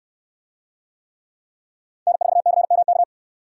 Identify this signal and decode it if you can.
Morse code, heard as a keyed tone, about 69 wpm, 700 Hz. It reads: N4CMK